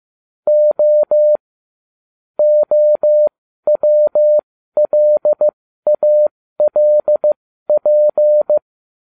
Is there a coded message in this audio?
Yes: Morse code from 0.5 to 8.6 s, heard as a keyed tone, about 15 wpm, spaced as for 12 wpm, 600 Hz. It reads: O OWLALP